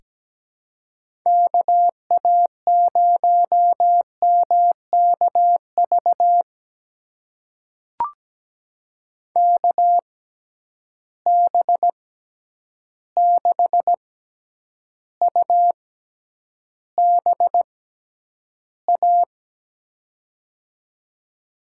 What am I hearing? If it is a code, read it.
Morse code, heard as a keyed tone, about 17 wpm, spaced as for 9 wpm, 700 Hz. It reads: KA0MKV KB6UBA
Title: USA Call Signs - 6 Characters - 6x Character Spacing - 17wpm